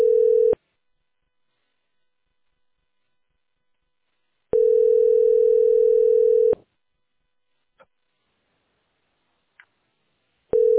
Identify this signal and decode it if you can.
Ringback tone (North America), 440 + 480 Hz.